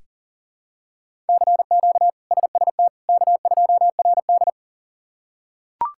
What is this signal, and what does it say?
Morse code, heard as a keyed tone, about 37 wpm, 700 Hz. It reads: CQ SST K2RD
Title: K1USN SST QSO Practice - 40wpm